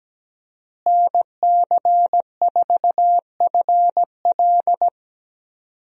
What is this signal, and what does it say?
Morse code, heard as a keyed tone, about 17 wpm, 700 Hz. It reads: NC4FL
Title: USA Call Signs - 5 Characters - Repeated 2x - 17wpm